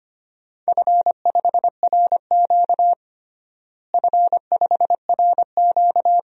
Morse code, heard as a keyed tone, about 25 wpm, 700 Hz. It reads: F5RQ F5RQ
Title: DXCC-100 Call Signs - 4 Characters - Repeated 2x - 25wpm